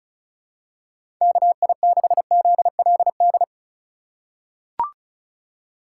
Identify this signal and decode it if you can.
Morse code, heard as a keyed tone, about 34 wpm, 690 Hz. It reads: KI6ZLD
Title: USA Call Signs - 3x Character Spacing - 35wpm